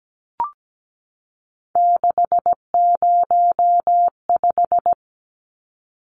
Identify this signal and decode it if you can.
Morse code, heard as a keyed tone, about 17 wpm, 700 Hz. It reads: 605
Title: Numbers - 3-Digit 17wpm